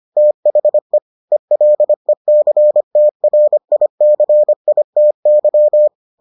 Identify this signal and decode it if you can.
Morse code, heard as a keyed tone, about 25 wpm, 600 Hz. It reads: THE ELECTRICITY